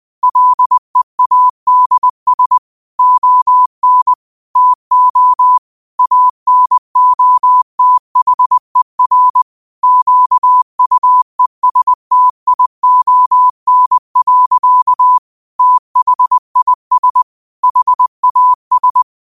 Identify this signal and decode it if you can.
Morse code, heard as a keyed tone, about 20 wpm, 1000 Hz. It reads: LEADS ON TO ANOTHER QUESTION. THIS HAS
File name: 20wpm